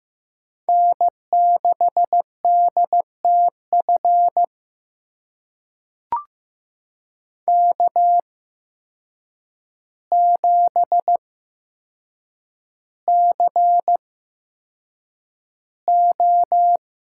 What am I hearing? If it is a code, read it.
Morse code, heard as a keyed tone, about 15 wpm, spaced as for 7 wpm, 700 Hz. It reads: N6DTF K7CO
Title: USA Call Signs - 5 Characters - 8x Character Spacing - 15wpm